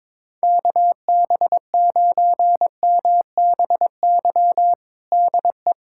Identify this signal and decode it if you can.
Morse code, heard as a keyed tone, about 22 wpm, 700 Hz. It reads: KB9MBY DE
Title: USA Callsign DE Callsign - 6 Characters - 22wpm